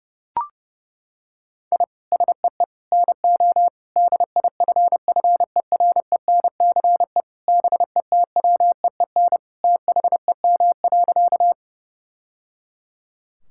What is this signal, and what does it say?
Morse code, heard as a keyed tone, about 30 wpm, 700 Hz. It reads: I SEE NO DIFFERENCE BETWEEN THEM.
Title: Sentences from Top 1000 Words - Review Part 5 - 30wpm